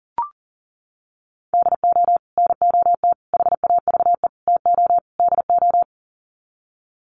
Morse code, heard as a keyed tone, about 40 wpm, 700 Hz. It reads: DO NOT HAVE TO DO
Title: Sets of 5 Words 40wpm